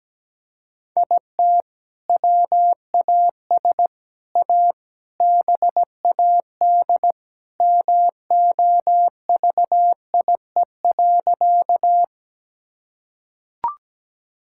Morse code, heard as a keyed tone, about 17 wpm, 700 Hz. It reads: IT WAS A BAD MOVIE.
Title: Sentences from Top 1000 Words - Review Part 5 - 17wpm